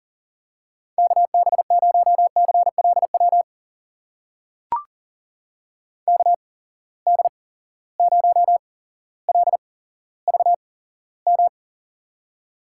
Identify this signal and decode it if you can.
Morse code, heard as a keyed tone, about 40 wpm, spaced as for 17 wpm, 700 Hz. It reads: KB0CLW KD0LVM
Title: USA Call Signs - 6 Characters - 8x Character Spacing - 40wpm